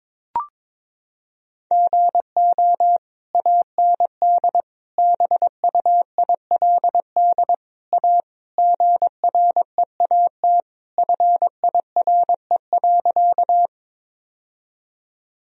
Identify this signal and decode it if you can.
Morse code, heard as a keyed tone, about 22 wpm, 700 Hz. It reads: GO AND BUILD A GREAT FIRE.
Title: Sentences from Top 1000 Words - Review Part 1 - 22wpm